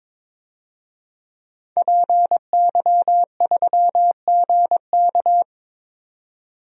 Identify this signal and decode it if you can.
Morse code, heard as a keyed tone, about 22 wpm, 700 Hz. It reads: PY3GK
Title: DXCC-100 Call Signs - 5 Characters - 22wpm